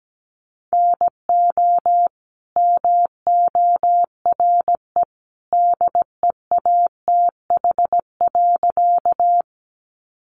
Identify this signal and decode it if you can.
Morse code, heard as a keyed tone, about 17 wpm, 700 Hz. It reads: NO MORE DEATH.
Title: Sentences from Top 1100 Words 17wpm